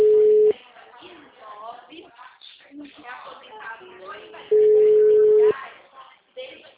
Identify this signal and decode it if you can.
Ringback tone (Europe, 425 Hz), with speech.